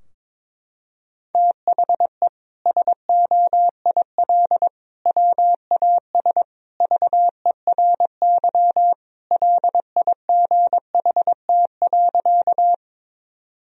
Morse code, heard as a keyed tone, about 22 wpm, 700 Hz. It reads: THE SOIL WAS VERY LIGHT.
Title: Sentences from Top 1000 Words - Review Part 4 - 22wpm